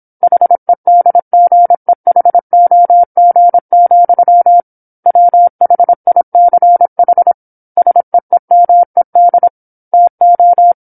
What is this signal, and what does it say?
Morse code, heard as a keyed tone, about 27 wpm, 700 Hz. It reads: HEDGEHOG, WHICH SEEMED TO
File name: 08-Chapter-08 26 WPM